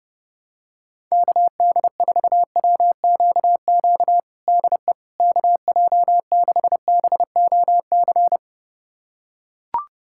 Morse code, heard as a keyed tone, about 30 wpm, 700 Hz. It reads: KD4WQQ DE KJ6BOC
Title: USA Callsign DE Callsign - 6 Characters - 30wpm